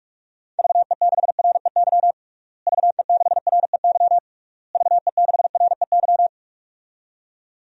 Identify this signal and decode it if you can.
Morse code, heard as a keyed tone, about 46 wpm, 700 Hz. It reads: VE6REY VE6REY VE6REY
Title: DXCC-100 Call Signs - 6 Characters - Repeated 3x - 45wpm